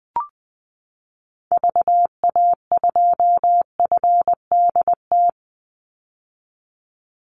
Morse code, heard as a keyed tone, about 20 wpm, 700 Hz. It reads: VA2FDT
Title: Call Signs - 20wpm